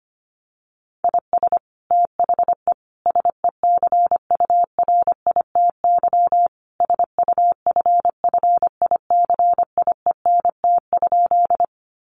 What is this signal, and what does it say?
Morse code, heard as a keyed tone, about 25 wpm, 700 Hz. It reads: IS THE SECURITY SUFFICIENT?
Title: Sentences from Top 1400 Words 25wpm